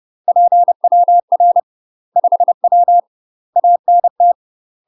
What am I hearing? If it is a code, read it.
Morse code, heard as a keyed tone, about 30 wpm, 700 Hz. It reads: PWR 5W ANT
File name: UK MASTER QSOs TRAINING / 30WPM / UK MASTER QSOs 0015